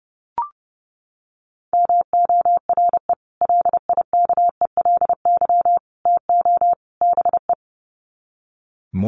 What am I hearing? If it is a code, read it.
Morse code, heard as a keyed tone, about 30 wpm, 700 Hz. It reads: MORE LIKELY TO BE
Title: Sets of 4 Words 30wpm